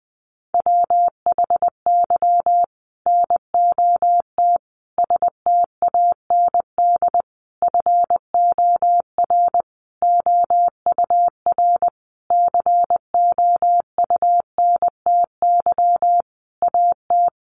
Morse code, heard as a keyed tone, about 20 wpm, 700 Hz. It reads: WHY NOT STAND FOR OUR COUNTY AT
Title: Sentences from Top 1500 Words 20wpm